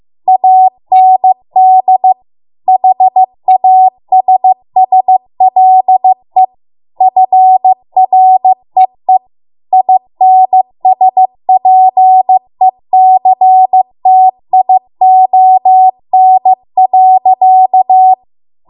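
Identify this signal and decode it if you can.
Morse code, heard as a keyed tone, about 15 wpm, 750 Hz. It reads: AND HASSLE FREE INSPECTION.
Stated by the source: Morse code practice at 15 wpm from the national amateur radio association's code archive.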